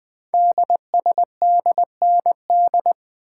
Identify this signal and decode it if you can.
Morse code, heard as a keyed tone, about 20 wpm, 700 Hz. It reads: DSDND
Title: Code Groups - 5 Letters - 20wpm